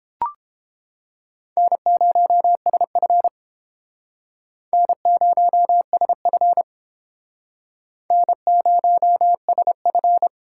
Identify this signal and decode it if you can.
Morse code, heard as a keyed tone, about 29 wpm, 700 Hz. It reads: N0SF N0SF N0SF